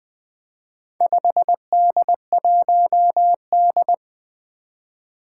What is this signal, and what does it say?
Morse code, heard as a keyed tone, about 20 wpm, 700 Hz. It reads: 5D1D
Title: Four Challenging Letter-Number Combinations 20wpm